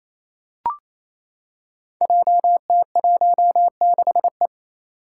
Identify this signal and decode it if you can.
Morse code, heard as a keyed tone, about 27 wpm, 700 Hz. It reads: JT16E
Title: Five Challenging Letter-Number Combinations 28wpm